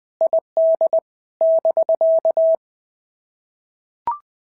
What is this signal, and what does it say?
Morse code, heard as a keyed tone, about 19 wpm, 650 Hz. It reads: ID <BK>
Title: POTA QSO Practice - 20wpm